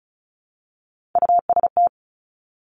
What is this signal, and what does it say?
Morse code, heard as a keyed tone, about 36 wpm, 700 Hz. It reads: UST
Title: Top Three Letter Combinations in Words 35wpm